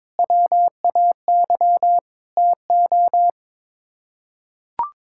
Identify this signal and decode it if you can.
Morse code, heard as a keyed tone, about 22 wpm, 700 Hz. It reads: WAY TO
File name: practice / 22wpm / Sets of 6 Words - 22wpm